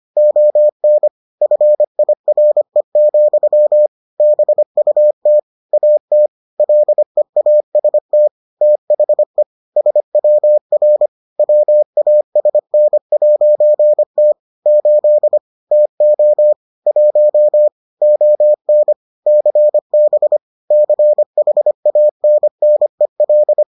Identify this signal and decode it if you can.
Morse code, heard as a keyed tone, about 25 wpm, 600 Hz. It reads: ON FIRE, BUT AT LEAST THE SWR WASN'T 8 TO 1 ON CB CHANNEL